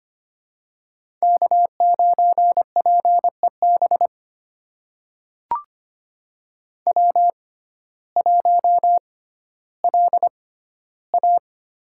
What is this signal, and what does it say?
Morse code, heard as a keyed tone, about 25 wpm, spaced as for 14 wpm, 700 Hz. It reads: K9PEB W1LA
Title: USA Call Signs - 5 Characters - 6x Character Spacing - 25wpm